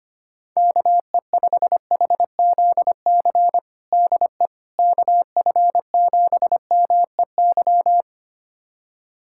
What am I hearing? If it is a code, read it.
Morse code, heard as a keyed tone, about 25 wpm, 700 Hz. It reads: KE5HZC DE KF7MEY
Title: USA Callsign DE Callsign - 6 Characters - 25wpm